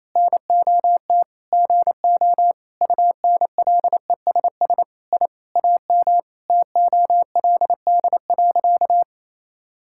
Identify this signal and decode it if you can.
Morse code, heard as a keyed tone, about 27 wpm, 700 Hz. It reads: NOT GO UNLESS I AM TOLD.
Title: Sentences from Top 1000 Words - Review Part 6 - 28wpm